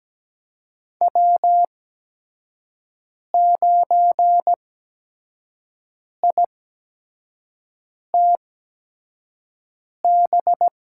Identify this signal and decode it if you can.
Morse code, heard as a keyed tone, about 17 wpm, spaced as for 5 wpm, 700 Hz. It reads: W9ITB
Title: USA Call Signs - 8x Character Spacing - 17wpm